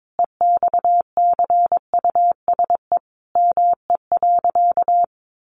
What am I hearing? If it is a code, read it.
Morse code, heard as a keyed tone, about 22 wpm, 700 Hz. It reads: EXCUSE ME.
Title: Sentences from Top 700 Words 22wpm